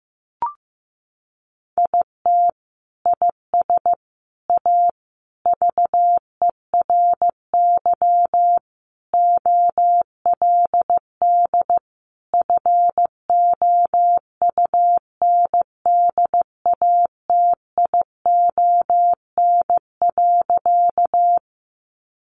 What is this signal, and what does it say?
Morse code, heard as a keyed tone, about 15 wpm, 700 Hz. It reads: IT IS A VERY OLD FOUNDATION.
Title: Sentences from Top 1200 Words 15wpm